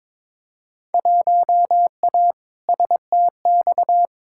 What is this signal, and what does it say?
Morse code, heard as a keyed tone, about 22 wpm, 700 Hz. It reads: 1A STX
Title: Field Day Exchanges - 22wpm